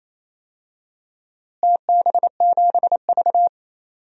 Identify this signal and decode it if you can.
Morse code, heard as a keyed tone, about 28 wpm, 700 Hz. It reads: TB7V
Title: DXCC-100 Call Signs - 4 Characters - Repeated 3x - 28wpm